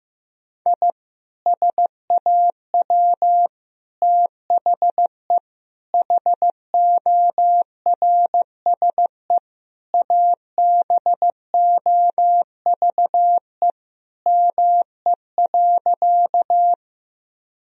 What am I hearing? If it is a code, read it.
Morse code, heard as a keyed tone, about 16 wpm, 700 Hz. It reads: I SAW THE HORSE ABOVE ME.
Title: Sentences from Top 1000 Words - Review Part 2 - 15wpm